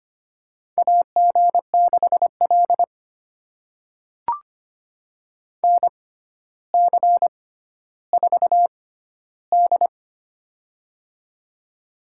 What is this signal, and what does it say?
Morse code, heard as a keyed tone, about 25 wpm, spaced as for 13 wpm, 700 Hz. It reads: AG6L NC4D
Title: USA Call Signs - 4 Characters - 6x Character Spacing - 25wpm